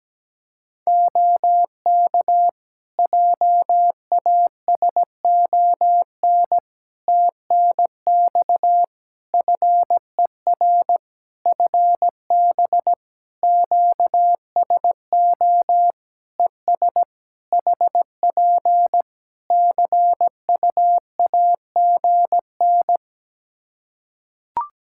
Morse code, heard as a keyed tone, about 17 wpm, 700 Hz. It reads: OK JASON TNX FER FB QSO ES HP CUAGN
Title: Practice QSO - Exchange 4 - Ending - 17wpm